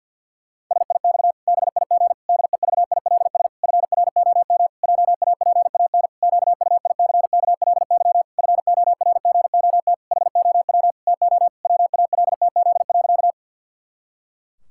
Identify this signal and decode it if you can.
Morse code, heard as a keyed tone, about 52 wpm, 700 Hz. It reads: SIX BIG DEVILS FROM JAPAN QUICKLY FORGOT HOW TO WALTZ.
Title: Pangram Challenge v2 - 50wpm